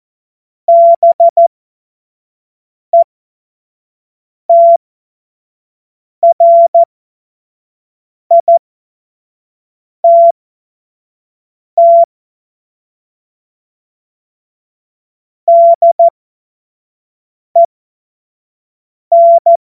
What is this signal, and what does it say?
Morse code, heard as a keyed tone, about 14 wpm, spaced as for 5 wpm, 680 Hz. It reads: BETRITT DEN
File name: Morsekurs / Witze / Witz 01. Juni 2021, 25 F70